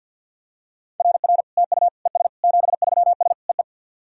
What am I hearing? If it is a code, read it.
Morse code, heard as a keyed tone, about 50 wpm, 690 Hz. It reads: RR TU ES 73S EE
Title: SOTA QSO Practice 50wpm